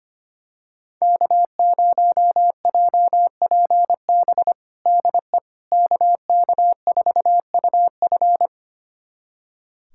Morse code, heard as a keyed tone, about 25 wpm, 700 Hz. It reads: K0JPB DE KK4UF
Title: USA Callsign DE Callsign - 5 Characters - 25wpm